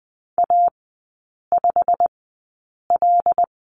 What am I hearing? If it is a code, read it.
Morse code, heard as a keyed tone, about 20 wpm, spaced as for 8 wpm, 700 Hz. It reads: A5L
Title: 093 Lesson 16e - Letter c - Three Characters 20wpm